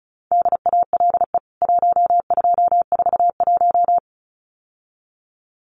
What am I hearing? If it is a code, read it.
Morse code, heard as a keyed tone, about 34 wpm, 700 Hz. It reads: DALE 1241
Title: CWT Exchange 35wpm